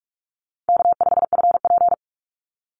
Morse code, heard as a keyed tone, about 44 wpm, 700 Hz. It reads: K5FP
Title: Call Signs - 45wpm